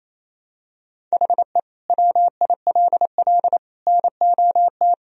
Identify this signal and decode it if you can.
Morse code, heard as a keyed tone, about 28 wpm, 700 Hz. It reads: HE WILL NOT
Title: Sentences from Top 1000 Words - Review Part 1 - 28wpm